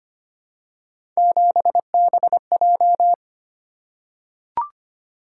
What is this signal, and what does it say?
Morse code, heard as a keyed tone, about 25 wpm, 700 Hz. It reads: 7BJ
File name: Three Letter-Number Combinations - 25wpm